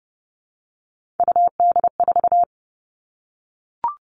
Morse code, heard as a keyed tone, about 30 wpm, 700 Hz. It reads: UD4
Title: All Cut Numbers - 3-Digit 30wpm